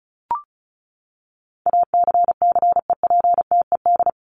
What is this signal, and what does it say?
Morse code, heard as a keyed tone, about 34 wpm, 700 Hz. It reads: ACCEPTED
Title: ED Suffix - Top 100 Words with Base Words 35wpm